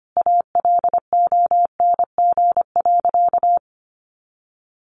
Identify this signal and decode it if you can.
Morse code, heard as a keyed tone, about 25 wpm, 700 Hz. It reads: ALONG.
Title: Sentences from Top 1500 Words 25wpm